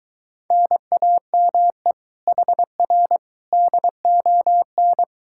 Morse code, heard as a keyed tone, about 23 wpm, 700 Hz. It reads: NAME HR DON